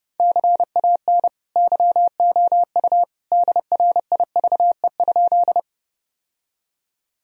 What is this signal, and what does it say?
Morse code, heard as a keyed tone, about 30 wpm, 700 Hz. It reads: CAN YOU DRIVE?
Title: Sentences from Top 1000 Words - Review Part 2 - 30wpm